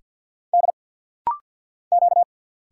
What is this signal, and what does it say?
Morse code, heard as a keyed tone, about 52 wpm, 700 Hz. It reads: D Q
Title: Single Letters - Rapid-Fire 50wpm